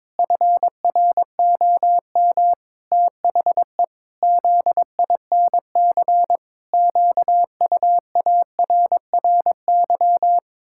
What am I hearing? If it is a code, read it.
Morse code, heard as a keyed tone, about 22 wpm, 700 Hz. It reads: FROM THE ZINC QUARRY
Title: Pangram Challenge v2 - 22wpm